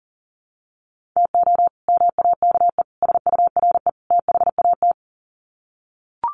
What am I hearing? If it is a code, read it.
Morse code, heard as a keyed tone, about 38 wpm, 700 Hz. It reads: TO MAKE SURE THAT
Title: Sets of 4 Words 40wpm